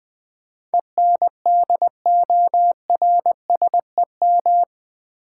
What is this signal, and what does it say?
Morse code, heard as a keyed tone, about 20 wpm, 700 Hz. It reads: ENDORSEM
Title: Build A Word - 4 Letters At A Time - 11-Letter Words - 20wpm